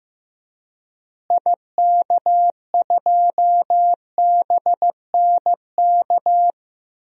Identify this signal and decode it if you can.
Morse code, heard as a keyed tone, about 15 wpm, 700 Hz. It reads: IK2BNK
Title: DXCC-100 Call Signs - 6 Characters - Repeated 2x - 15wpm